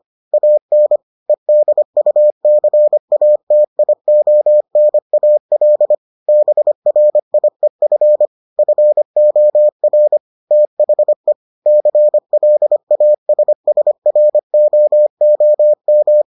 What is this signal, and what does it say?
Morse code, heard as a keyed tone, about 25 wpm, 600 Hz. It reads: AN EDUCATIONAL BRIEF FOR THE CLASSROOM